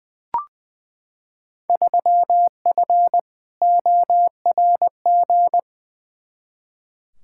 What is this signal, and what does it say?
Morse code, heard as a keyed tone, about 20 wpm, 700 Hz. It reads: 3F ORG
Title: Field Day Exchanges - 20wpm